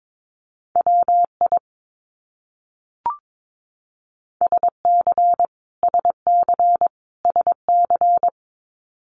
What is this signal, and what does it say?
Morse code, heard as a keyed tone, about 22 wpm, 700 Hz. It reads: WI SC SC SC
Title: US States Repeated 3x 22wpm